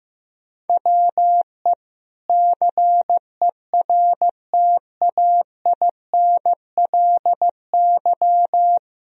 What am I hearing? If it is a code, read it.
Morse code, heard as a keyed tone, about 15 wpm, 700 Hz. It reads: WE CERTAINLY
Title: Sentences from Top 1000 Words - Review Part 4 - 15wpm